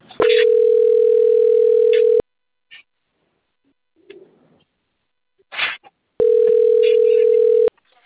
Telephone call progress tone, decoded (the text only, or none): ringback tone